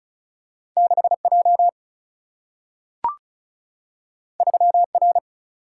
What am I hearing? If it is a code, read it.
Morse code, heard as a keyed tone, about 36 wpm, 700 Hz. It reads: 6 J 3 R
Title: Single-Letter Single-Number 35wpm